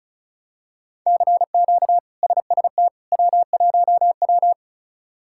Morse code, heard as a keyed tone, about 36 wpm, 700 Hz. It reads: CQ SST W1W
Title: K1USN SST QSO Practice - 35wpm